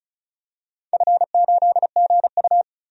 Morse code, heard as a keyed tone, about 36 wpm, 690 Hz. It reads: F8GU
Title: DXCC-100 Call Signs - 4 Characters - 35wpm